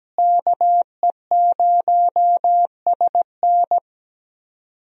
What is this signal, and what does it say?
Morse code, heard as a keyed tone, about 17 wpm, 700 Hz. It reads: KE0SN
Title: USA Call Signs - 5 Characters - Repeated 3x - 17wpm